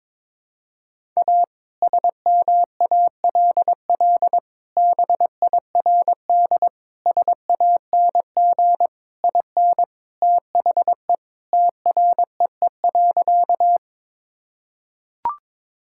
Morse code, heard as a keyed tone, about 21 wpm, 700 Hz. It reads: A SMALL BIRD SANG IN THE TREE.
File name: practice / 22wpm / Sentences with 2-Word Phrases - 22wpm